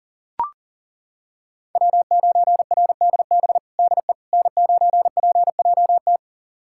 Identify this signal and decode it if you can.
Morse code, heard as a keyed tone, about 40 wpm, 700 Hz. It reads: W9RDB DE N9PJT
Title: USA Callsign DE Callsign - 5 Characters - 40wpm